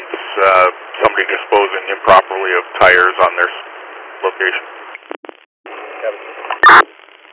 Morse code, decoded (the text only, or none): none